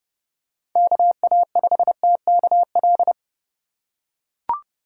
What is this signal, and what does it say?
Morse code, heard as a keyed tone, about 30 wpm, 700 Hz. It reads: KA5TKL
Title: USA Call Signs - 4x Character Spacing - 30wpm